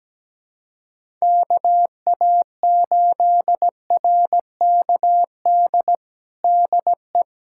Morse code, heard as a keyed tone, about 17 wpm, 700 Hz. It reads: KA8RKD DE
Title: USA Callsign DE Callsign - 6 Characters - 17wpm